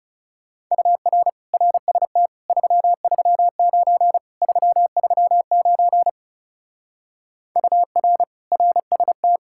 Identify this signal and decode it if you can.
Morse code, heard as a keyed tone, about 36 wpm, 700 Hz. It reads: UR RST 339 339 UR RST
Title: Speed-Racing Common QSO Phrases Encore - 30wpm